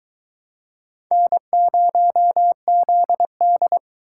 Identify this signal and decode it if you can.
Morse code, heard as a keyed tone, about 22 wpm, 700 Hz. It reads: N0ZD